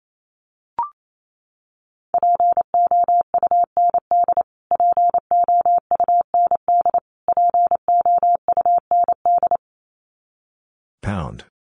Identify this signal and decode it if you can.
Morse code, heard as a keyed tone, about 28 wpm, 700 Hz. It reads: POUND POUND POUND
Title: Top 500 Words Repeated 28wpm